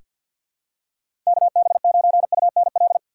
Morse code, heard as a keyed tone, about 48 wpm, 700 Hz. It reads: KB9UNL
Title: USA Call Signs - 4x Character Spacing - 50wpm